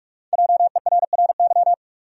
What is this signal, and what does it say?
Morse code, heard as a keyed tone, about 43 wpm, 700 Hz. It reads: JERRY
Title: USA Common Names - Max 5 Letters - 45wpm